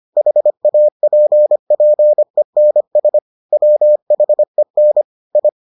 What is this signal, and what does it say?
Morse code, heard as a keyed tone, about 25 wpm, 600 Hz. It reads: HAPPENS WHEN I